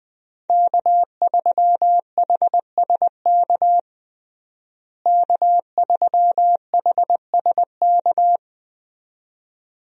Morse code, heard as a keyed tone, about 20 wpm, 700 Hz. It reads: K3HSK K3HSK
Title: USA Call Signs - 5 Characters - Repeated 2x - 20wpm